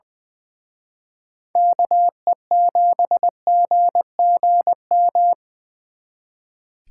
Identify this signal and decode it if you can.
Morse code, heard as a keyed tone, about 23 wpm, 700 Hz. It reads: KE7GGM
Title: Speed Racing - USA Call Signs 20wpm